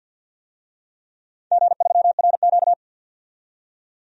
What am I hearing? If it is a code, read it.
Morse code, heard as a keyed tone, about 51 wpm, 700 Hz. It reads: G3RQ
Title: DXCC-100 Call Signs - 4 Characters - 50wpm